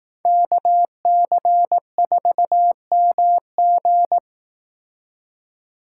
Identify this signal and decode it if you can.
Morse code, heard as a keyed tone, about 18 wpm, 700 Hz. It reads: KC4MG
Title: Speed Racing - USA 2x2 Call Signs 15wpm